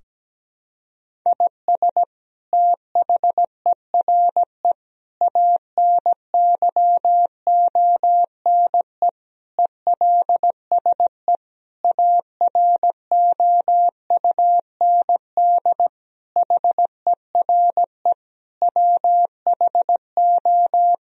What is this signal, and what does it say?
Morse code, heard as a keyed tone, about 17 wpm, 700 Hz. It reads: IS THERE ANYONE ELSE AROUND HERE WHO
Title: Sentences from Top 1000 Words - Review Part 3 - 17wpm